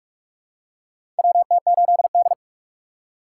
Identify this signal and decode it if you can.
Morse code, heard as a keyed tone, about 43 wpm, 700 Hz. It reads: WT8D
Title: USA Call Signs - 4 Characters - 2x Character Spacing - 45wpm